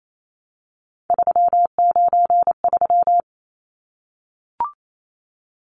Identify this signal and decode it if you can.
Morse code, heard as a keyed tone, about 28 wpm, 700 Hz. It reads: 393
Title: Numbers - 3-Digit 28wpm